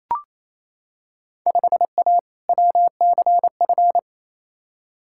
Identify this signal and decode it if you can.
Morse code, heard as a keyed tone, about 28 wpm, 700 Hz. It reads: 5A WCF
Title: Field Day Exchanges Encore - 28wpm